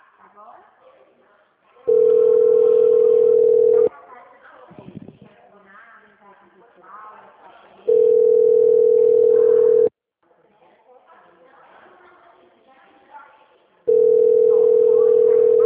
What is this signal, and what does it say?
Ringback tone (North America), 440 + 480 Hz, with speech.
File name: out- -2003- - - .25140